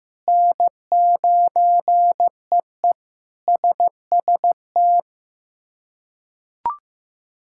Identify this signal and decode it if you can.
Morse code, heard as a keyed tone, about 16 wpm, 700 Hz. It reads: N9EE SST
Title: K1USN SST QSO Practice - 15wpm